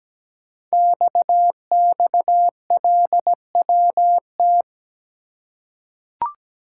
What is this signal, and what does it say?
Morse code, heard as a keyed tone, about 17 wpm, 700 Hz. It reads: XXLWT
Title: Code Groups - 5 Letters - 17wpm